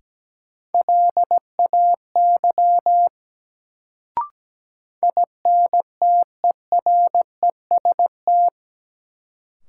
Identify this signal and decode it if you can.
Morse code, heard as a keyed tone, about 17 wpm, 700 Hz. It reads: LAY INTEREST
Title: Sentences from Top 400 Words - New Word Drill - 17wpm